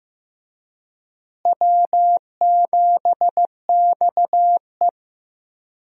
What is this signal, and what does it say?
Morse code, heard as a keyed tone, about 15 wpm, 700 Hz. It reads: W7XE
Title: USA Call Signs - 4 Characters - Repeated 3x - 15wpm